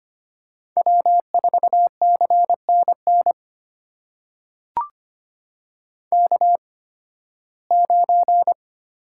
Morse code, heard as a keyed tone, about 25 wpm, spaced as for 16 wpm, 700 Hz. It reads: W4CNN K9
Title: USA Call Signs - 5 Characters - 8x Character Spacing - 25wpm